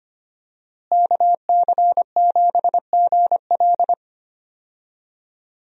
Morse code, heard as a keyed tone, about 25 wpm, 700 Hz. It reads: KC7GL
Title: USA Call Signs - 5 Characters - Repeated 2x - 25wpm